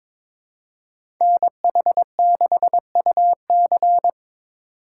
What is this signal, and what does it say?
Morse code, heard as a keyed tone, about 23 wpm, 700 Hz. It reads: NH6UC
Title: USA Call Signs - 5 Characters - 3x Character Spacing - 22wpm